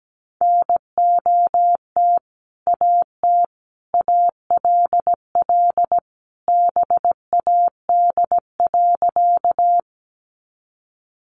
Morse code, heard as a keyed tone, about 17 wpm, 700 Hz. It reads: NOT AT ALL BAD.
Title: Sentences from Top 700 Words 17wpm